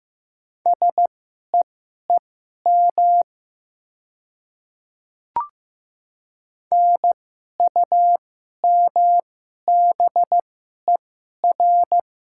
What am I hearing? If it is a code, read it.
Morse code, heard as a keyed tone, about 15 wpm, spaced as for 11 wpm, 700 Hz. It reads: SEEM NUMBER
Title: 2x Character Spacing - Top 300 Words - 15wpm